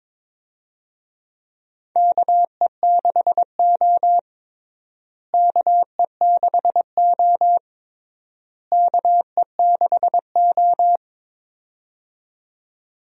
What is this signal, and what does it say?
Morse code, heard as a keyed tone, about 22 wpm, 700 Hz. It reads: KE6O KE6O KE6O
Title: USA Call Signs - 4 Characters - Repeated 3x - 22wpm